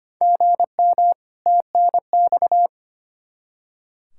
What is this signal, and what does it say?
Morse code, heard as a keyed tone, about 25 wpm, 700 Hz. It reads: GM TNX
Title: Practice QSO - Exchange 1 - Beginning - 25wpm